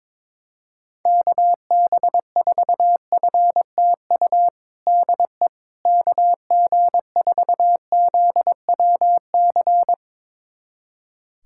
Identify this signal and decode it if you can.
Morse code, heard as a keyed tone, about 22 wpm, 700 Hz. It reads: KB4FTU DE KG4ZWC
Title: USA Callsign DE Callsign - 6 Characters - 22wpm